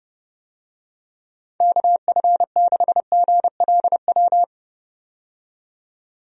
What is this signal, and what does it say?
Morse code, heard as a keyed tone, about 30 wpm, 700 Hz. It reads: KF6GLW
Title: USA Call Signs - 6 Characters - 30wpm